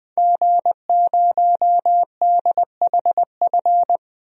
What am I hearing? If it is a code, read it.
Morse code, heard as a keyed tone, about 20 wpm, 700 Hz. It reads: G0DHF
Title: DXCC-100 Call Signs - 5 Characters - Repeated 2x - 20wpm